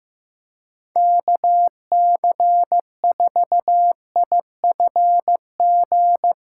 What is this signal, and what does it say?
Morse code, heard as a keyed tone, about 15 wpm, 700 Hz. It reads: KC4IFG
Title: USA Callsign DE Callsign - 6 Characters - 15wpm